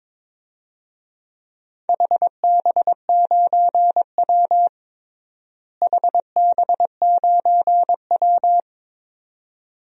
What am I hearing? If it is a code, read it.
Morse code, heard as a keyed tone, about 21 wpm, 700 Hz. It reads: HB9W HB9W
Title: DXCC-100 Call Signs - 4 Characters - Repeated 2x - 22wpm